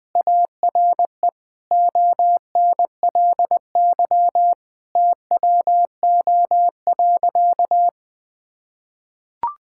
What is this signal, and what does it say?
Morse code, heard as a keyed tone, about 21 wpm, 700 Hz. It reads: ARE ONLY TWO.
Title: Sentences from Top 1000 Words - Review Part 5 - 20wpm